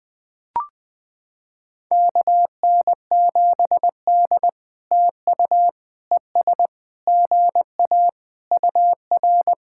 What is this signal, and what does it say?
Morse code, heard as a keyed tone, about 21 wpm, 700 Hz. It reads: KN7D TU ES GA UR
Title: POTA QSO Practice - 20wpm